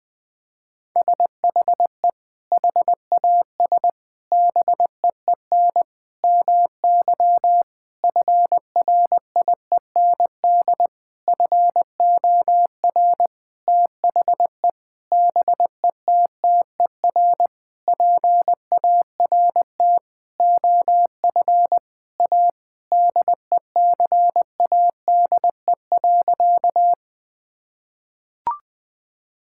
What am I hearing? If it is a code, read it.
Morse code, heard as a keyed tone, about 21 wpm, 700 Hz. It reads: SHE HAS BEEN MY FRIEND FOR THE BETTER PART OF A DECADE.
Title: Sentences with 6-Word Phrases - Spoken Beginning - 20wpm